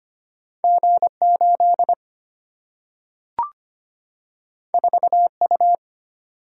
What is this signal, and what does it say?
Morse code, heard as a keyed tone, about 25 wpm, 700 Hz. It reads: G 8 4 U
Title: Single-Letter Single-Number 25wpm